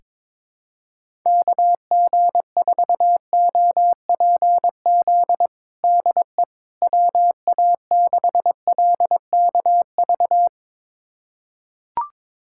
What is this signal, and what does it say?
Morse code, heard as a keyed tone, about 22 wpm, 700 Hz. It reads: KG4OPZ DE WA6LKV
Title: USA Callsign DE Callsign - 6 Characters - 22wpm